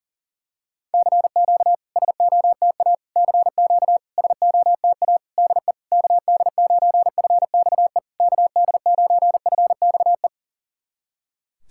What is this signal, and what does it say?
Morse code, heard as a keyed tone, about 40 wpm, 700 Hz. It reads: CQ SOTA CQ SOTA DE KD9FXE KD9FXE
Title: SOTA QSO Practice 40wpm